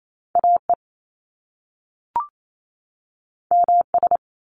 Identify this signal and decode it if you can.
Morse code, heard as a keyed tone, about 28 wpm, 700 Hz. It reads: A E M S
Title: Top Two Letter Combinations in Words 28wpm